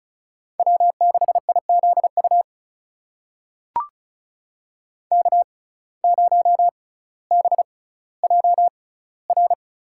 Morse code, heard as a keyed tone, about 34 wpm, spaced as for 18 wpm, 700 Hz. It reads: W6IZU K0BJR
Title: USA Call Signs - 6x Character Spacing - 35wpm